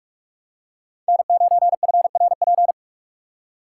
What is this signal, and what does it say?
Morse code, heard as a keyed tone, about 43 wpm, 700 Hz. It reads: N9FRP